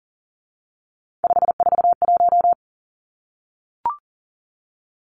Morse code, heard as a keyed tone, about 40 wpm, 700 Hz. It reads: 541
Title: Numbers - 3-Digit 40wpm